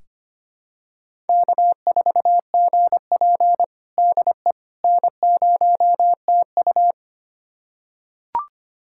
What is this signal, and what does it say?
Morse code, heard as a keyed tone, about 25 wpm, 700 Hz. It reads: K4GP DE N0TU
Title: USA Callsign DE Callsign - 4 Characters - 25wpm